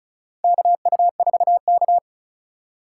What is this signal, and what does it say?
Morse code, heard as a keyed tone, about 36 wpm, 700 Hz. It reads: KU4K
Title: USA Call Signs - 4 Characters - 3x Character Spacing - 35wpm